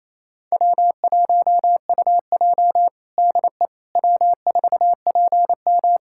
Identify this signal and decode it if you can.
Morse code, heard as a keyed tone, about 28 wpm, 700 Hz. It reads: W1UJ DE W4PM
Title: K1USN SST QSO Practice 28wpm